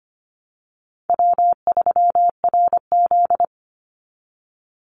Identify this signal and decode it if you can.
Morse code, heard as a keyed tone, about 25 wpm, 700 Hz. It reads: W3RZ
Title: Easy Call Signs Repeated 2x - 25wpm